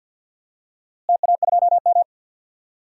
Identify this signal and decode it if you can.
Morse code, heard as a keyed tone, about 52 wpm, 700 Hz. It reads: TA2M
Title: DXCC-100 Call Signs - 4 Characters - Repeated 3x - 50wpm